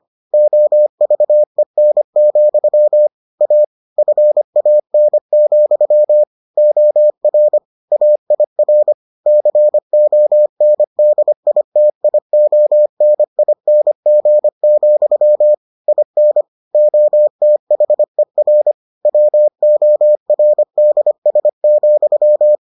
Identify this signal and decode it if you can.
Morse code, heard as a keyed tone, about 25 wpm, 600 Hz. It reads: OVEN, A FAN, OR AIR CONDITIONING, IN OTHER WORDS,